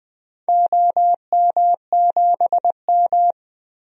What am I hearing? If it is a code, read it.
Morse code, heard as a keyed tone, about 20 wpm, 700 Hz. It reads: OM7M
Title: DXCC-100 Call Signs - 4 Characters - 20wpm